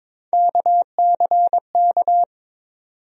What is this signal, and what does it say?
Morse code, heard as a keyed tone, about 22 wpm, 700 Hz. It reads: KCK
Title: USA Call Sign Postfixes - 3 Letters 22wpm